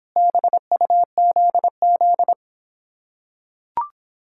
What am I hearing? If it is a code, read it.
Morse code, heard as a keyed tone, about 26 wpm, 700 Hz. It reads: BUZZ